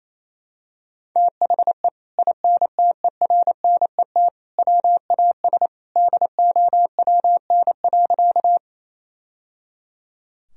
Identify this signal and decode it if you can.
Morse code, heard as a keyed tone, about 28 wpm, 700 Hz. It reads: THE INTERNET WAS DOWN.
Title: Sentences from Top 1000 Words - Review Part 4 - 28wpm